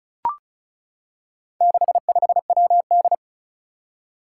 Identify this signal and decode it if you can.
Morse code, heard as a keyed tone, about 36 wpm, 700 Hz. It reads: 65WD
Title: Four Challenging Letter-Number Combinations 35wpm